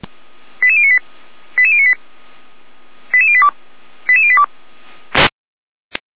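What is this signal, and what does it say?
Five-tone selective call (ZVEI), heard as quick runs of five tones, tones of about 70 ms. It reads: 88098 88098 88082 88082
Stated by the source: ZVEI five-tone selective call received on SDR.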